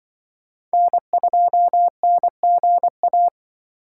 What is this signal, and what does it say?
Morse code, heard as a keyed tone, about 24 wpm, 700 Hz. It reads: N2NGA
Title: Speed Racing - USA 1x3 Call Signs 20wpm